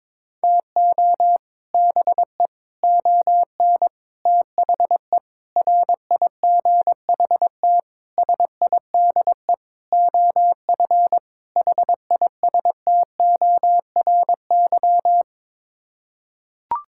Morse code, heard as a keyed tone, about 22 wpm, 700 Hz. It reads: TO BE ON THE RIGHT SIDE OF HISTORY
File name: practice / 22wpm / Sets of 8 Words - 22wpm